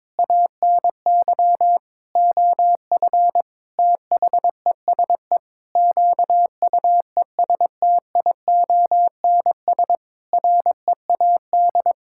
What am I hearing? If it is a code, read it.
Morse code, heard as a keyed tone, about 22 wpm, 700 Hz. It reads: ANY OF THESE QUESTIONS READ
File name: Sentences with 9-Word Phrases - 22wpm